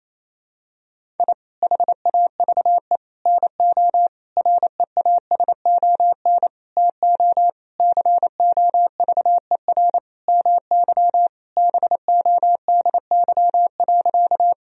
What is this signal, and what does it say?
Morse code, heard as a keyed tone, about 28 wpm, 700 Hz. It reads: I HAVE NO REASON TO COVER MY BODY.
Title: Sentences from Top 1000 Words - Review Part 1 - 28wpm